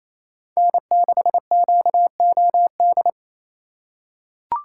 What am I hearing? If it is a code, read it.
Morse code, heard as a keyed tone, about 28 wpm, 700 Hz. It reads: N6QOD
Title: USA Call Signs - 5 Characters - 4x Character Spacing - 28wpm